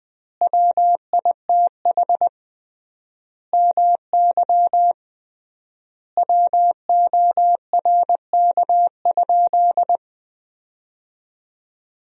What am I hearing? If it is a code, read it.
Morse code, heard as a keyed tone, about 20 wpm, 700 Hz. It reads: WITH MY WORK?
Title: 3x Word-Spacing - Sentences from Top 500 Words - 20wpm